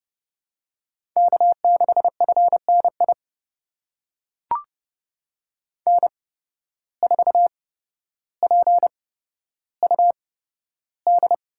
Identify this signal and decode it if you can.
Morse code, heard as a keyed tone, about 30 wpm, spaced as for 13 wpm, 700 Hz. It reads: K6FNI N4PUD
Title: USA Call Signs - 5 Characters - 8x Character Spacing - 30wpm